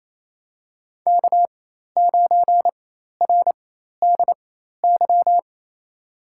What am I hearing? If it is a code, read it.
Morse code, heard as a keyed tone, about 28 wpm, spaced as for 13 wpm, 700 Hz. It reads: K9RDY
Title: USA Call Signs - 5 Characters - 4x Character Spacing - 28wpm